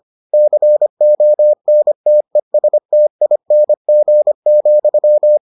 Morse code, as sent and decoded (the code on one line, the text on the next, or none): -.-. --- -. - . ... - .. -. --. --..--
CONTESTING,